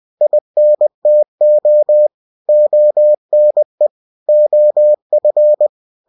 Morse code, heard as a keyed tone, about 20 wpm, 600 Hz. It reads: INTO ONE OF
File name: cw-books / winnie-the-pooh / winnie-20wpm-page-0025